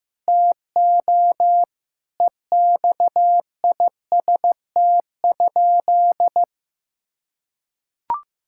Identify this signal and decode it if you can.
Morse code, heard as a keyed tone, about 16 wpm, 700 Hz. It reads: TO EXIST?
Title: Sentences from Top 1000 Words - Review Part 5 - 15wpm